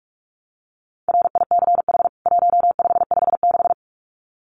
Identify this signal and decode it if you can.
Morse code, heard as a keyed tone, about 44 wpm, 700 Hz. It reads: RICH 1556
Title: CWT Exchange 45wpm